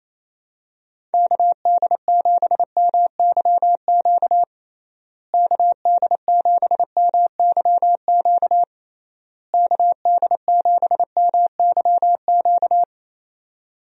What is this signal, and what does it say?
Morse code, heard as a keyed tone, about 27 wpm, 700 Hz. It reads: KD7MYQ KD7MYQ KD7MYQ
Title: USA Call Signs - 6 Characters - Repeated 3x - 28wpm